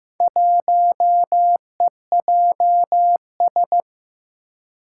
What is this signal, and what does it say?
Morse code, heard as a keyed tone, about 15 wpm, 700 Hz. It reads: 1EJS
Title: Four Challenging Letter-Number Combinations 15wpm